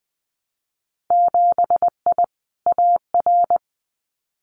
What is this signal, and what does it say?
Morse code, heard as a keyed tone, about 20 wpm, 700 Hz. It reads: 7I AR
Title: Winter Field Day Exchanges Encore 20wpm